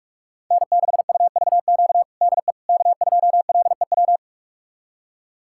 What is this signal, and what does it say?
Morse code, heard as a keyed tone, about 45 wpm, 700 Hz. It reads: N6UVQ DE K2LEW
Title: USA Callsign DE Callsign - 5 Characters - 45wpm